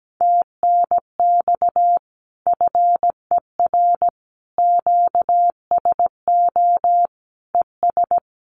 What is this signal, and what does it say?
Morse code, heard as a keyed tone, about 17 wpm, 700 Hz. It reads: TNX FER QSO ES
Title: Common QSO Phrases Encore 17wpm